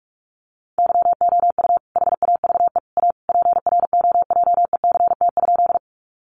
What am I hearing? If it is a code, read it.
Morse code, heard as a keyed tone, about 44 wpm, 700 Hz. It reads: YOU HAVE A PROJECT?
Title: Sentences from Top 1400 Words 45wpm